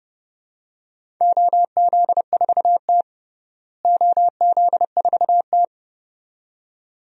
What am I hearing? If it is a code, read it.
Morse code, heard as a keyed tone, about 30 wpm, 700 Hz. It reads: OZ4T OZ4T
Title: DXCC-100 Call Signs - 4 Characters - Repeated 2x - 30wpm